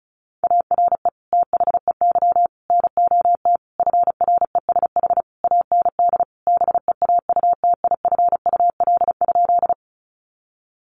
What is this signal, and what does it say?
Morse code, heard as a keyed tone, about 34 wpm, 700 Hz. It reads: ARE THEY NOT FRESH AND BEAUTIFUL?
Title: Sentences from Top 1200 Words 35wpm